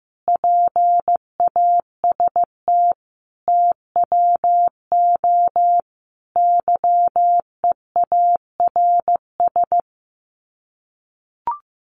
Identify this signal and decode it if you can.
Morse code, heard as a keyed tone, about 15 wpm, 700 Hz. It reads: PAST TWO YEARS
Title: Sets of 5 Words 15wpm